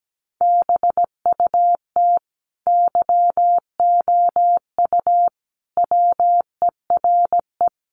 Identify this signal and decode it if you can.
Morse code, heard as a keyed tone, about 17 wpm, 700 Hz. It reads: BUT YOU WERE
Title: Sentences from Top 800 Words 17wpm